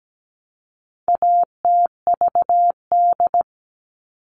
Morse code, heard as a keyed tone, about 17 wpm, 700 Hz. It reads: ATVD
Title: All Cut Numbers - 4-Digit 17wpm